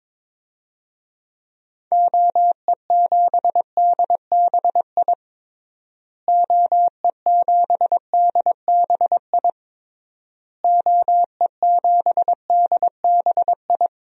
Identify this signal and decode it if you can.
Morse code, heard as a keyed tone, about 22 wpm, 700 Hz. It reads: OE7DBI OE7DBI OE7DBI
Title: DXCC-100 Call Signs - 6 Characters - Repeated 3x - 22wpm